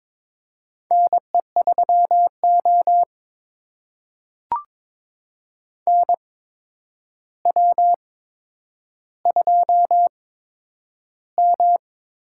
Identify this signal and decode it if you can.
Morse code, heard as a keyed tone, about 22 wpm, spaced as for 9 wpm, 700 Hz. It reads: NE3O NW2M
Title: USA Call Signs - 4 Characters - 8x Character Spacing - 22wpm